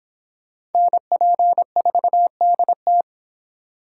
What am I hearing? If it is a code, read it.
Morse code, heard as a keyed tone, about 25 wpm, 700 Hz. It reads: NP4DT